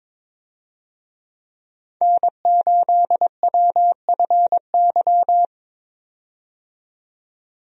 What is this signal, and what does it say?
Morse code, heard as a keyed tone, about 22 wpm, 700 Hz. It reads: N8WFY
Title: USA Call Signs - 5 Characters - 22wpm